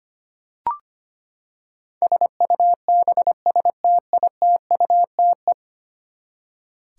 Morse code, heard as a keyed tone, about 25 wpm, 700 Hz. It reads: SUBSTITUTE
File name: Ten Letter Words - 25wpm